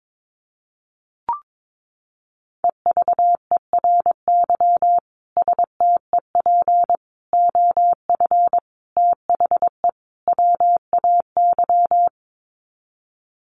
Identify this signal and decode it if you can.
Morse code, heard as a keyed tone, about 22 wpm, 700 Hz. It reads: EVERY STEP OF THE WAY
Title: Sets of 5 Words Encore2 22wpm